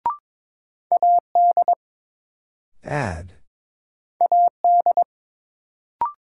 Morse code, heard as a keyed tone, about 22 wpm, 700 Hz. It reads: AD AD
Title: Sentences from Top 1300 Words - New Word Drill Alt - 22wpm